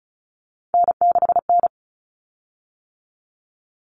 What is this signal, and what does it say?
Morse code, heard as a keyed tone, about 34 wpm, 700 Hz. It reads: N6N
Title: Common Cut Numbers - 3-Digit 35wpm